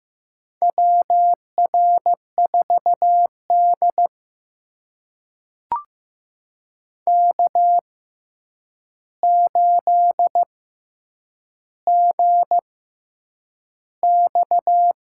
Morse code, heard as a keyed tone, about 15 wpm, spaced as for 8 wpm, 700 Hz. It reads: WR4D K8GX
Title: USA Call Signs - 4 Characters - 6x Character Spacing - 15wpm